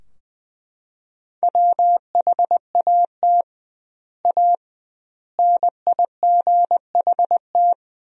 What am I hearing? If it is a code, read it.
Morse code, heard as a keyed tone, about 20 wpm, 700 Hz. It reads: WHAT A NIGHT
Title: 2x Word-Spacing - Sentences from Top 300 Words - 20wpm